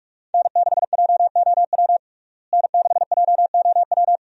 Morse code, heard as a keyed tone, about 42 wpm, 700 Hz. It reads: N6JOW N6JOW
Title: USA Call Signs - 5 Characters - Repeated 2x - 45wpm